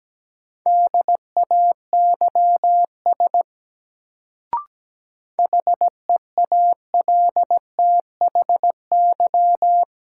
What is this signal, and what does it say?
Morse code, heard as a keyed tone, about 17 wpm, 700 Hz. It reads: DAYS HEALTHY
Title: Sentences from Top 1000 Words - Review Part 5 - New Word Drill - 17wpm